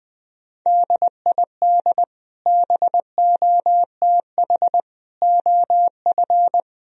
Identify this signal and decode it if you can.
Morse code, heard as a keyed tone, about 20 wpm, 700 Hz. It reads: DID BOTH OF
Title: Sentences from Top 1000 Words - Review Part 2 - 20wpm